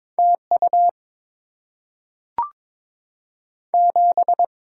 Morse code, heard as a keyed tone, about 22 wpm, 700 Hz. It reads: T U 7
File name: Top 20 QSO Elements - 8x Character Spacing - 22wpm